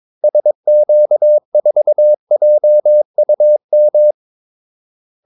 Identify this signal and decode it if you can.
Morse code, heard as a keyed tone, about 22 wpm, 600 Hz. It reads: SQ4JUM
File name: Callsigns-22wpm